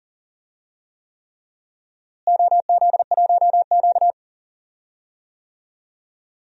Morse code, heard as a keyed tone, about 40 wpm, 700 Hz. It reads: OZ1Q
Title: DXCC-100 Call Signs - 4 Characters - Repeated 3x - 40wpm